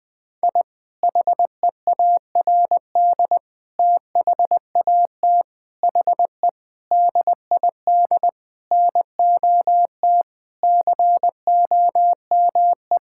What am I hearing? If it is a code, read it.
Morse code, heard as a keyed tone, about 20 wpm, 700 Hz. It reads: I HEARD THAT HE DID NOT COME
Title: Sentences with 4-Word Phrases - 20wpm